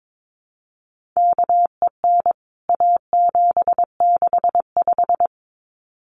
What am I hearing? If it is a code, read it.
Morse code, heard as a keyed tone, about 22 wpm, 700 Hz. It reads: KEN A765
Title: CWT Exchange 22wpm